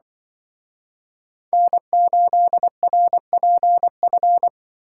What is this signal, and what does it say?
Morse code, heard as a keyed tone, about 25 wpm, 700 Hz. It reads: N8RPF
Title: Speed Racing - USA 1x3 Call Signs 20wpm